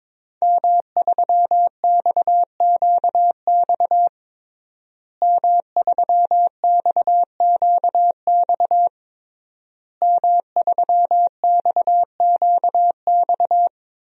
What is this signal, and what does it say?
Morse code, heard as a keyed tone, about 22 wpm, 700 Hz. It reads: M3XQX M3XQX M3XQX
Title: DXCC-100 Call Signs - 5 Characters - Repeated 3x - 22wpm